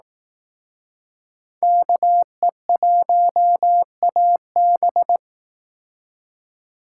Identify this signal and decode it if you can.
Morse code, heard as a keyed tone, about 19 wpm, 700 Hz. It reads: KE1AB
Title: Speed Racing - USA 2x2 Call Signs 15wpm